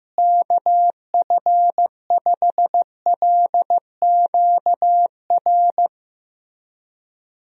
Morse code, heard as a keyed tone, about 15 wpm, 700 Hz. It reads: KF5LQR
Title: USA Call Signs - 6 Characters - Repeated 3x - 15wpm